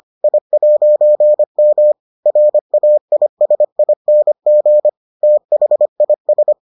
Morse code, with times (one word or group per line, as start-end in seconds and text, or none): 0.2-1.9 I'M
2.3-4.9 RAISING
5.2-6.5 THIS